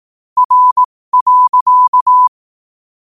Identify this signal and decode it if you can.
Morse code, heard as a keyed tone, about 18 wpm, spaced as for 15 wpm, 1000 Hz. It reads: R.
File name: qotdfiles / 15wpm